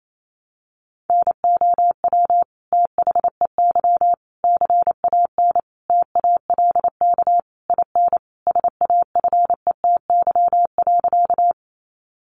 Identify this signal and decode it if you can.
Morse code, heard as a keyed tone, about 28 wpm, 700 Hz. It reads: NOW THEY CAN TALK IN SAFETY.
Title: Sentences from Top 700 Words 28wpm